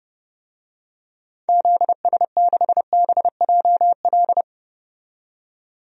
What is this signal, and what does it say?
Morse code, heard as a keyed tone, about 30 wpm, 700 Hz. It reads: ZS6BJL
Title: DXCC-100 Call Signs - 6 Characters - 30wpm